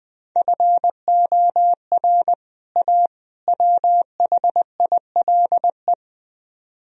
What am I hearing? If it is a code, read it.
Morse code, heard as a keyed tone, about 20 wpm, 700 Hz. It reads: FOR A WHILE
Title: Speed-Racing - Sets of 3 Words 17wpm